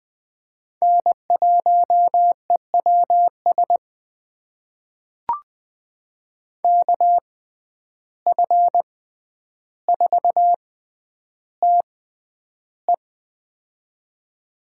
Morse code, heard as a keyed tone, about 20 wpm, spaced as for 10 wpm, 700 Hz. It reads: N1EWS KF4TE
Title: USA Call Signs - 5 Characters - 6x Character Spacing - 20wpm